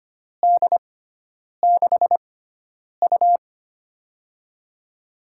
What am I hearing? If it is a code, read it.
Morse code, heard as a keyed tone, about 25 wpm, spaced as for 9 wpm, 700 Hz. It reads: D6U
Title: USA Call Signs - 4 Characters - 6x Character Spacing - 25wpm